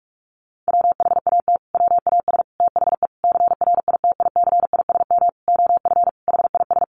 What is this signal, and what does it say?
Morse code, heard as a keyed tone, about 44 wpm, 700 Hz. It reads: WHAT WAS THE CRITICISM OF HIS
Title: Sentences from Top 1300 Words 45wpm